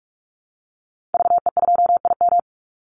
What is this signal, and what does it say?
Morse code, heard as a keyed tone, about 44 wpm, 700 Hz. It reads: VE2IM
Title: Call Signs Repeated 2x - 45wpm